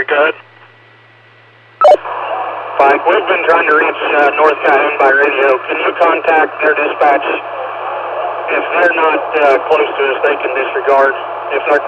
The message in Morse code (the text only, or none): none